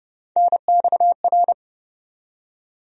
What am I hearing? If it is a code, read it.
Morse code, heard as a keyed tone, about 29 wpm, 700 Hz. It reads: NXR
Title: USA Call Sign Postfixes - 3 Letters 30wpm